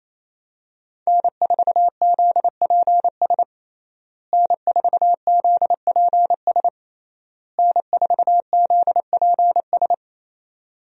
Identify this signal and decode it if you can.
Morse code, heard as a keyed tone, about 27 wpm, 700 Hz. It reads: N4ZPS N4ZPS N4ZPS
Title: USA Call Signs - 5 Characters - Repeated 3x - 28wpm